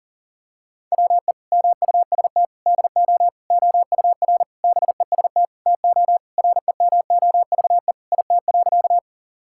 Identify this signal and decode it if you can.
Morse code, heard as a keyed tone, about 37 wpm, 700 Hz. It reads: WE MUST DO OUR BEST TO REMOVE IT.
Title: Sentences from Top 1000 Words - Review Part 5 - 40wpm